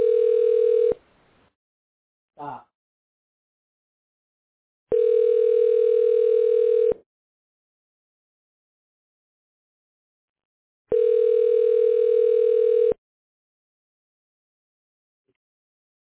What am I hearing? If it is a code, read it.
Ringback tone (North America), 440 + 480 Hz.